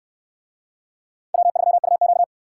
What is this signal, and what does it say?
Morse code, heard as a keyed tone, about 64 wpm, 700 Hz. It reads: W3UQ